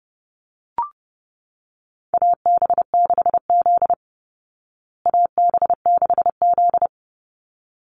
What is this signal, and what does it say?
Morse code, heard as a keyed tone, about 30 wpm, 700 Hz. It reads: AB6Z AB6Z
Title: Call Signs Repeated 3x - 30wpm